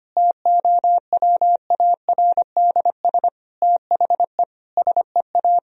Morse code, heard as a keyed tone, about 25 wpm, 700 Hz. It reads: TOWARDS THE SEA
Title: Sentences from Top 1000 Words - Review Part 2 - 25wpm